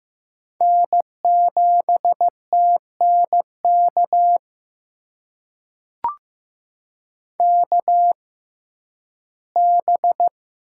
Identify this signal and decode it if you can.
Morse code, heard as a keyed tone, about 15 wpm, spaced as for 11 wpm, 700 Hz. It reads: N7TNK KB
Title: USA Call Signs - 6x Character Spacing - 15wpm